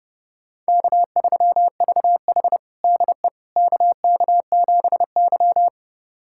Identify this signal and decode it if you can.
Morse code, heard as a keyed tone, about 30 wpm, 700 Hz. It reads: K3VH DE KK7Y
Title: USA Callsign DE Callsign - 4 Characters - 30wpm